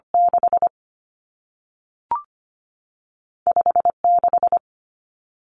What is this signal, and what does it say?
Morse code, heard as a keyed tone, about 25 wpm, 700 Hz. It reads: 6 56
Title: Numbers - 2-Digit 25wpm